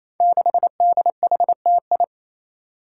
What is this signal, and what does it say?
Morse code, heard as a keyed tone, about 28 wpm, 700 Hz. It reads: 6DHTI
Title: Five Challenging Letter-Number Combinations 28wpm